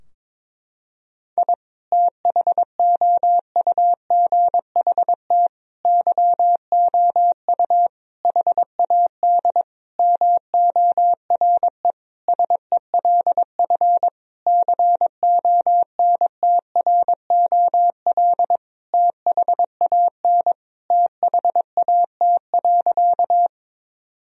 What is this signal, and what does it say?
Morse code, heard as a keyed tone, about 22 wpm, 700 Hz. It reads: I THOUGHT YOU HAD MORE SELF CONTROL THAN THAT.
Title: Sentences from Top 1000 Words - Review Part 3 - 22wpm